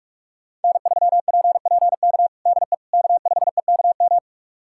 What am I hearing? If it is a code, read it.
Morse code, heard as a keyed tone, about 46 wpm, 700 Hz. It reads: N3PPK DE K5EKM
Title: USA Callsign DE Callsign - 5 Characters - 45wpm